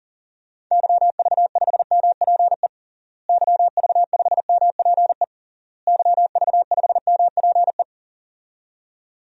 Morse code, heard as a keyed tone, about 40 wpm, 700 Hz. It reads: YV5MPE YV5MPE YV5MPE
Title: DXCC-100 Call Signs - 6 Characters - Repeated 3x - 40wpm